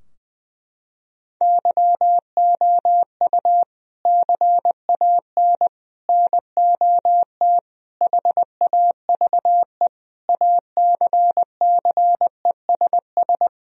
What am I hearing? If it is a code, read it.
Morse code, heard as a keyed tone, about 21 wpm, 700 Hz. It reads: YOU CAN NOT HAVE ACCESS
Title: Sentences from Top 1000 Words - Review Part 4 - 20wpm